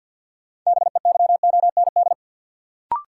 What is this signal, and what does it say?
Morse code, heard as a keyed tone, about 53 wpm, 700 Hz. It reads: BEYOND
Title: Sentences from Top 1100 Words - New Word Drill Alt - 50wpm